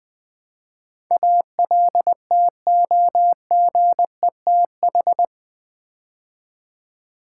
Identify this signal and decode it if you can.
Morse code, heard as a keyed tone, about 20 wpm, 700 Hz. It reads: ALTOGETH
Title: Build A Word - 4 Letters At A Time - 10-Letter Words - 20wpm